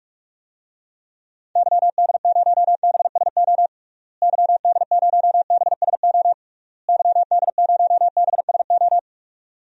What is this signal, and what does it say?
Morse code, heard as a keyed tone, about 46 wpm, 700 Hz. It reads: YD0BSO YD0BSO YD0BSO
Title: DXCC-100 Call Signs - 6 Characters - Repeated 3x - 45wpm